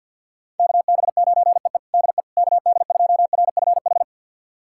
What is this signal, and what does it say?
Morse code, heard as a keyed tone, about 46 wpm, 700 Hz. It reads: KB9EE DE KD2RFH
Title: USA Callsign DE Callsign 50wpm